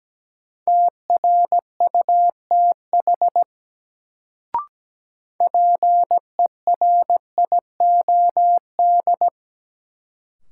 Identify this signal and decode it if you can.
Morse code, heard as a keyed tone, about 17 wpm, 700 Hz. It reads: TRUTH PERIOD
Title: Sentences from Top 800 Words - New Word Drill - 17wpm